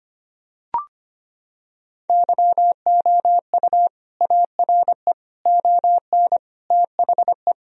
Morse code, heard as a keyed tone, about 25 wpm, 700 Hz. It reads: YOU ARE ON THE
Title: Sentences from Top 1000 Words - Review Part 2 - 25wpm